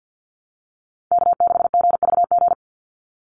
Morse code, heard as a keyed tone, about 50 wpm, 700 Hz. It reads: K6GVG
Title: Call Signs Repeated 3x - 50wpm